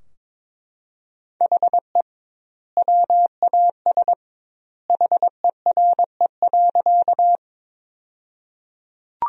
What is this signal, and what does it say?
Morse code, heard as a keyed tone, about 22 wpm, 700 Hz. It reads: HE WAS HERE.
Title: 2x Word-Spacing - Sentences from Top 200 Words - 22wpm